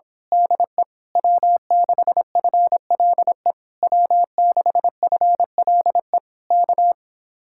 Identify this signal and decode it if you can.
Morse code, heard as a keyed tone, about 26 wpm, 700 Hz. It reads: DE W6FLE W6FLE K
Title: Speed-Racing Common QSO Phrases - 22wpm